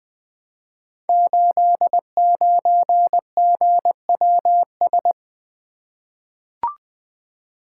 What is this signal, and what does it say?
Morse code, heard as a keyed tone, about 20 wpm, 700 Hz. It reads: 89GWS
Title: Code Groups - 5 Letter:Numbers - 20wpm